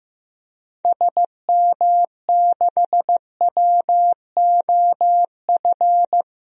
Morse code, heard as a keyed tone, about 15 wpm, 700 Hz. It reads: SM6WOF
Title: DXCC-100 Call Signs - 6 Characters - 15wpm